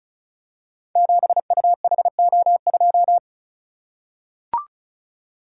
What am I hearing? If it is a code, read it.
Morse code, heard as a keyed tone, about 36 wpm, 690 Hz. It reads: 7UHO2
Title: Code Groups - 5 Letter:Numbers - 35wpm